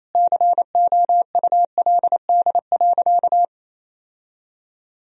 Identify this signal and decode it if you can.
Morse code, heard as a keyed tone, about 28 wpm, 700 Hz. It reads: COULD.
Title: Sentences from Top 1000 Words - Review Part 2 - 28wpm